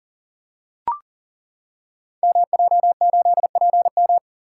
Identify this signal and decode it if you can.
Morse code, heard as a keyed tone, about 42 wpm, 700 Hz. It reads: MJ8PM
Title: Code Groups - 5 Letter:Numbers - 40wpm